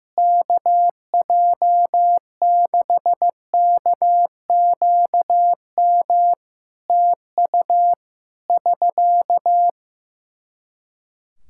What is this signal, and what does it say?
Morse code, heard as a keyed tone, about 14 wpm, 700 Hz. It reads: KJ6KQM TU <SK>
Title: Practice QSO - Exchange 4 - Ending - 15wpm